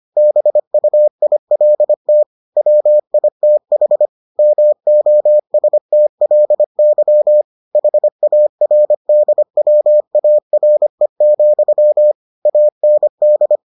Morse code, heard as a keyed tone, about 25 wpm, 600 Hz. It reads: BUILT WITH MOSTLY HARDWARE, AND